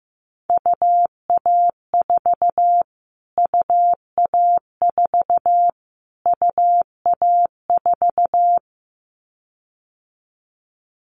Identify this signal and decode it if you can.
Morse code, heard as a keyed tone, about 15 wpm, 700 Hz. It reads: UA4 UA4 UA4
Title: Callsign Prefixes for Top 25 DXCC - Repeated 3x - 15wpm